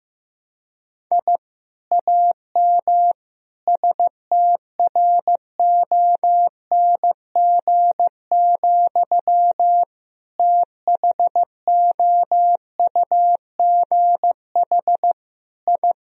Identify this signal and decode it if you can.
Morse code, heard as a keyed tone, about 14 wpm, 700 Hz. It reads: I AM STRONG, THOUGH I
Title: Sentences from Top 1000 Words - Review Part 3 - 15wpm